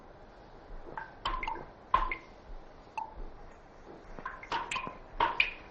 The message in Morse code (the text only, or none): none